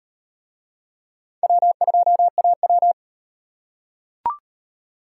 Morse code, heard as a keyed tone, about 34 wpm, 700 Hz. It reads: W2AW